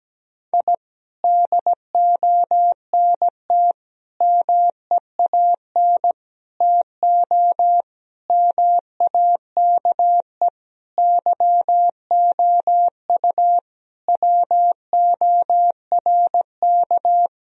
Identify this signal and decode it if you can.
Morse code, heard as a keyed tone, about 17 wpm, 700 Hz. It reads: I DONT MEAN TO MAKE YOU WORK
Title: Sentences from Top 1000 Words - Review Part 1 - 17wpm